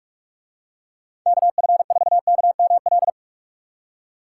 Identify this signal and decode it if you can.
Morse code, heard as a keyed tone, about 44 wpm, 700 Hz. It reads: KF4KML